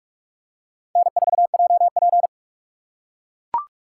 Morse code, heard as a keyed tone, about 42 wpm, 700 Hz. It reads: N4JP
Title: USA Call Signs - 4 Characters - 3x Character Spacing - 45wpm